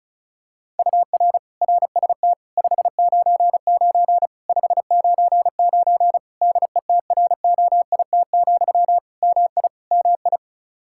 Morse code, heard as a keyed tone, about 36 wpm, 700 Hz. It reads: UR RST 599 599 DETROIT, MI MI
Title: Practice QSO - Exchange 1 - Beginning - 35wpm